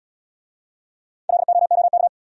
Morse code, heard as a keyed tone, about 67 wpm, 700 Hz. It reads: LOOK